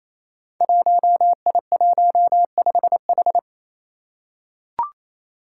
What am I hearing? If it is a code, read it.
Morse code, heard as a keyed tone, about 29 wpm, 700 Hz. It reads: 1I15H
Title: Five Challenging Letter-Number Combinations 28wpm